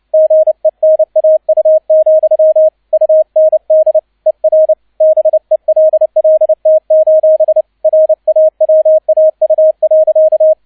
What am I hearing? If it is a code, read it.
Morse code, heard as a keyed tone, about 29 wpm, 630 Hz, with radio noise. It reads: GENAU, UND ER BELLT: RAWAU.